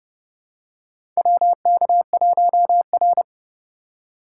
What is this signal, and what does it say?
Morse code, heard as a keyed tone, about 30 wpm, 700 Hz. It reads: WK1R
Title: USA Call Signs - 4 Characters - 3x Character Spacing - 30wpm